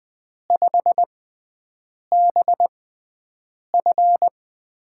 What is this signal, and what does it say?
Morse code, heard as a keyed tone, about 20 wpm, spaced as for 7 wpm, 700 Hz. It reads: 5BF
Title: USA Call Signs - 6x Character Spacing - 20wpm